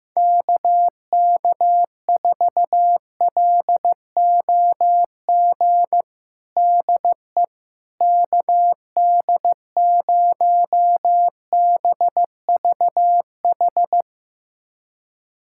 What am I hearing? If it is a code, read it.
Morse code, heard as a keyed tone, about 15 wpm, 700 Hz. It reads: KK4LOG DE KD0BVH
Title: USA Callsign DE Callsign - 6 Characters - 15wpm